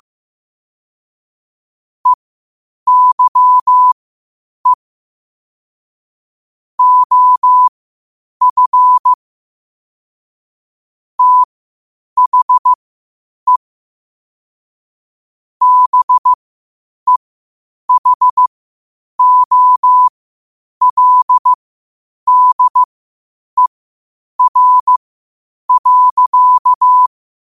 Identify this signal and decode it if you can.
Morse code, heard as a keyed tone, about 15 wpm, spaced as for 8 wpm, 1000 Hz. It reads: EYE OF THE BEHOLDER.
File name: qotdfiles / 8wpm